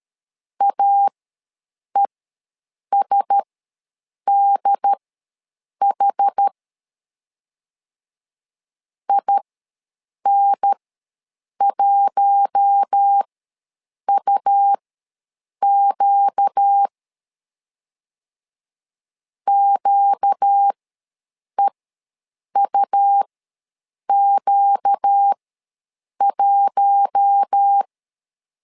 Morse code, heard as a keyed tone, about 13 wpm, spaced as for 7 wpm, 800 Hz. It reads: AESDH IN1UQ QEUQ1
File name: LEZIONI / 4 Q1S ws 12wpm 1827